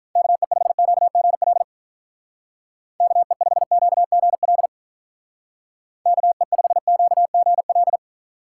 Morse code, heard as a keyed tone, about 44 wpm, 700 Hz. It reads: KE5QGL KE5QGL KE5QGL